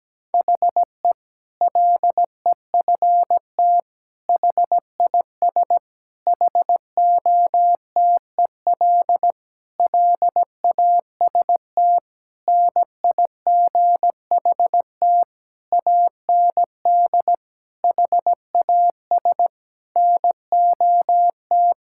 Morse code, heard as a keyed tone, about 17 wpm, 700 Hz. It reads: HE LEFT HIS HOTEL LAST NIGHT AND HAS NOT
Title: Sentences from Top 1600 Words 17wpm